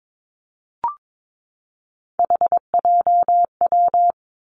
Morse code, heard as a keyed tone, about 22 wpm, 700 Hz. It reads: HJW
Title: Three Challenging Letter-Number Combinations 22wpm